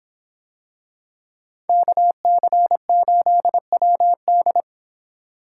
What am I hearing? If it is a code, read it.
Morse code, heard as a keyed tone, about 25 wpm, 700 Hz. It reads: KC8WD